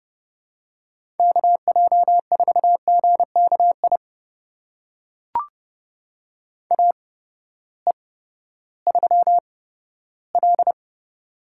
Morse code, heard as a keyed tone, about 29 wpm, spaced as for 15 wpm, 700 Hz. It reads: KJ4GKI AE3L
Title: USA Call Signs - 8x Character Spacing - 30wpm